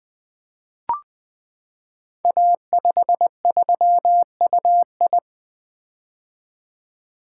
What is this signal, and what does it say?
Morse code, heard as a keyed tone, about 20 wpm, 700 Hz. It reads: A53UI
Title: Code Groups - 5 Letter:Numbers - 20wpm